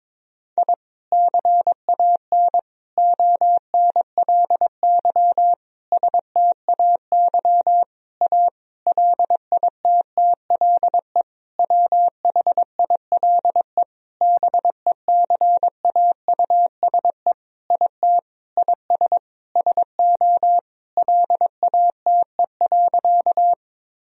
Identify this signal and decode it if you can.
Morse code, heard as a keyed tone, about 22 wpm, 700 Hz. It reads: I CAN ONLY STAY A LITTLE WHILE BECAUSE IT IS SO LATE.
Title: Sentences from Top 1000 Words - Review Part 3 - 22wpm